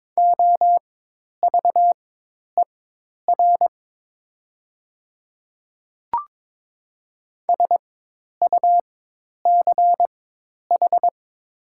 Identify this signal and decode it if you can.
Morse code, heard as a keyed tone, about 22 wpm, spaced as for 10 wpm, 700 Hz. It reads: OVER SUCH
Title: 4x Character Spacing - Top 300 Words - 22wpm